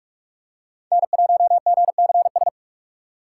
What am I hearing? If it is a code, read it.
Morse code, heard as a keyed tone, about 46 wpm, 700 Hz. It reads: N1GCS
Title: USA Call Signs - 5 Characters - 2x Character Spacing - 45wpm